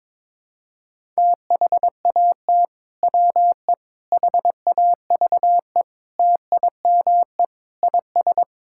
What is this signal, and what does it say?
Morse code, heard as a keyed tone, about 22 wpm, 700 Hz. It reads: THAT WE HAVE TIME IS
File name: practice / 22wpm / Sentences with 3-Word Phrases - 22wpm